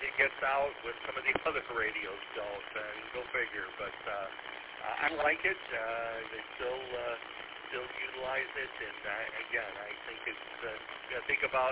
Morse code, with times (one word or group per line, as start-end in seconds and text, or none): none